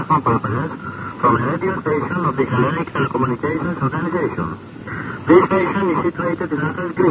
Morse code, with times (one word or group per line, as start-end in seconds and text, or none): none